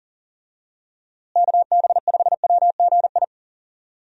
Morse code, heard as a keyed tone, about 40 wpm, 700 Hz. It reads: KB5WGI
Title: USA Call Signs - 6 Characters - Repeated 3x - 40wpm